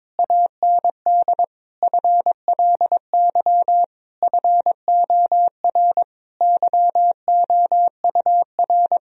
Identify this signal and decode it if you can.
Morse code, heard as a keyed tone, about 22 wpm, 700 Hz. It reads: AND FLY FOR YOUR
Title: Sentences from Top 1000 Words - Review Part 2 - 22wpm